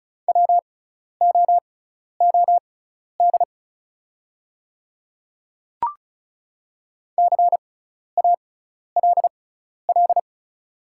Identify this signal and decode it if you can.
Morse code, heard as a keyed tone, about 36 wpm, spaced as for 12 wpm, 690 Hz. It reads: WOOD CALL
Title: 6x Character Spacing - Top 300 Words - 35wpm